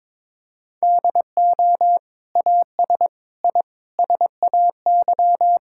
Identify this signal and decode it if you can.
Morse code, heard as a keyed tone, about 22 wpm, 700 Hz. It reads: DO AS I SAY
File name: practice / 22wpm / Sentences with 8-Word Phrases - 22wpm